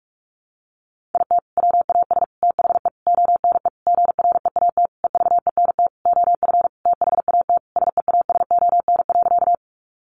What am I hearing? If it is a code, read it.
Morse code, heard as a keyed tone, about 44 wpm, 700 Hz. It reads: IT WAS THE ONE GREAT EVENT OF THAT SEASON.
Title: Sentences from Top 1400 Words 45wpm